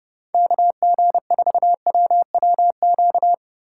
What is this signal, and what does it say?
Morse code, heard as a keyed tone, about 30 wpm, 700 Hz. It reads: KG4WWQ
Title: USA Call Signs - 6 Characters - 2x Character Spacing - 30wpm